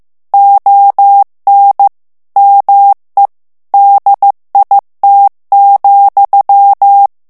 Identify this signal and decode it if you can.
Morse code, heard as a keyed tone, about 15 wpm, 800 Hz. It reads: ON ME DIT,
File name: FRANCAIS TEXTE CLAIR 2 VITESSE 14 MOTS MINUTES